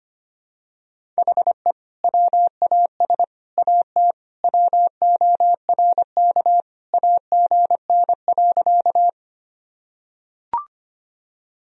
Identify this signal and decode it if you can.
Morse code, heard as a keyed tone, about 25 wpm, 700 Hz. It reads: HE WAS AT WORK AGN.
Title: Sentences from Top 200 Words and Abbreviations 25wpm